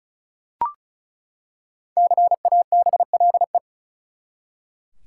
Morse code, heard as a keyed tone, about 34 wpm, 700 Hz. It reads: CABLE
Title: Five Letter Words - 35wpm